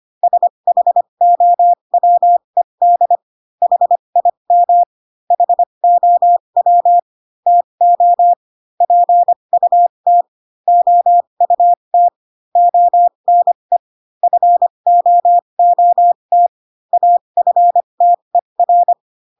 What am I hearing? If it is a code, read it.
Morse code, heard as a keyed tone, about 25 wpm, spaced as for 22 wpm, 700 Hz. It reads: SHOWED HIM HOW TO PUT OUT ONE FOOT AFTER
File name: Pinocchio / Pinocchio ch3 18 25 ik1hge